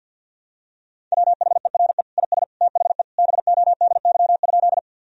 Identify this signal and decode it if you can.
Morse code, heard as a keyed tone, about 50 wpm, 700 Hz. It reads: WHERE IS THE BODY?
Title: Sentences from Top 1000 Words - Review Part 2 - 50wpm